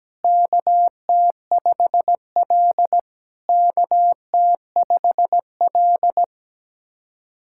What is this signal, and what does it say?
Morse code, heard as a keyed tone, about 17 wpm, 700 Hz. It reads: KT5L KT5L
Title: SOTA QSO Practice 17wpm